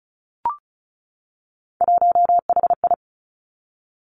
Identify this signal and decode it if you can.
Morse code, heard as a keyed tone, about 36 wpm, 700 Hz. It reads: 1HI
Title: Three Challenging Letter-Number Combinations 35wpm